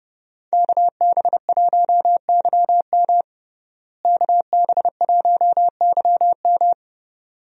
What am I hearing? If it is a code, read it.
Morse code, heard as a keyed tone, about 32 wpm, 700 Hz. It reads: KB1YM KB1YM
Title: USA Call Signs - 5 Characters - Repeated 2x - 30wpm